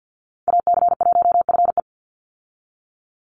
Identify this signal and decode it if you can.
Morse code, heard as a keyed tone, about 50 wpm, 700 Hz. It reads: AC1FE
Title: Call Signs Repeated 3x - 50wpm